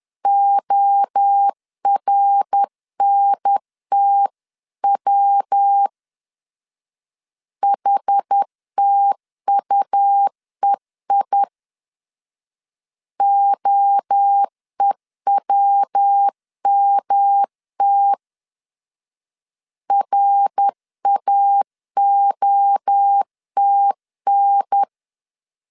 Morse code, heard as a keyed tone, about 10 wpm, 800 Hz. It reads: ORNTW HTUEI OEWMT RAOTN
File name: CW / DET / 1 3 1500C S B 10wpm 3335